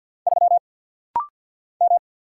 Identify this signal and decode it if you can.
Morse code, heard as a keyed tone, about 50 wpm, 690 Hz. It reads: 3 M
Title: Single Letter-Number Alt - Rapid-Fire 50wpm